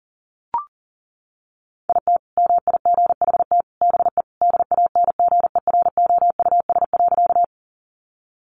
Morse code, heard as a keyed tone, about 40 wpm, 700 Hz. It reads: IT MIGHT BE DANGEROUS.
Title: Sentences from Top 1200 Words 40wpm